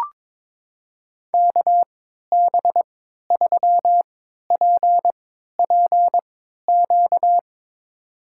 Morse code, heard as a keyed tone, about 22 wpm, spaced as for 13 wpm, 700 Hz. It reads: KB3PPQ
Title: USA Call Signs - 3x Character Spacing - 22wpm